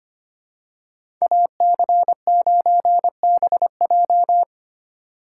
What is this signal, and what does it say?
Morse code, heard as a keyed tone, about 25 wpm, 700 Hz. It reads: AC9BJ
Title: USA Call Signs - 5 Characters - 2x Character Spacing - 25wpm